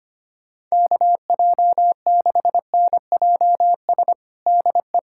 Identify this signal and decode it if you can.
Morse code, heard as a keyed tone, about 25 wpm, 700 Hz. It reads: KJ6NJS DE
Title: USA Callsign DE Callsign 25wpm